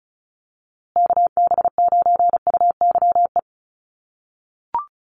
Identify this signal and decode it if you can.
Morse code, heard as a keyed tone, about 34 wpm, 700 Hz. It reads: KB9UYE
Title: USA Call Signs Repeated 3x 35wpm